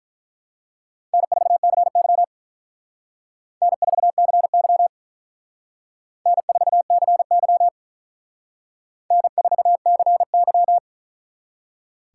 Morse code, heard as a keyed tone, about 45 wpm, 700 Hz. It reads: N4CY N4CY N4CY N4CY